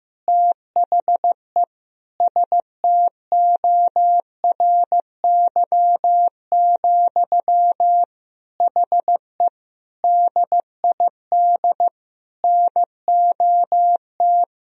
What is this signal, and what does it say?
Morse code, heard as a keyed tone, about 15 wpm, 700 Hz. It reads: THE STORY, HE DID NOT
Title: Sentences from Top 1000 Words - Review Part 3 - 15wpm